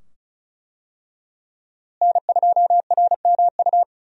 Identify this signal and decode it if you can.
Morse code, heard as a keyed tone, about 36 wpm, 700 Hz. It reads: N2RMU
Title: USA Call Signs - 5 Characters - Repeated 2x - 35wpm